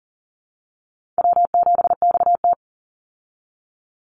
Morse code, heard as a keyed tone, about 40 wpm, 700 Hz. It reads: W7XT
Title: Easy Call Signs - 40wpm